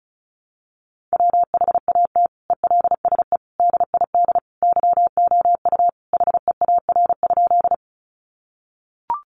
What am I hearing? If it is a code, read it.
Morse code, heard as a keyed tone, about 34 wpm, 700 Hz. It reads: WHAT ELSE DID YOU HEAR?
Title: Sentences from Top 800 Words 35wpm